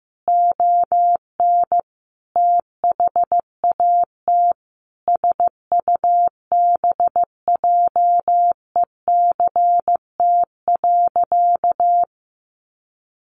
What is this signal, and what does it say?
Morse code, heard as a keyed tone, about 15 wpm, 700 Hz. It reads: ON THAT SUBJECT.
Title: Sentences from Top 800 Words 15wpm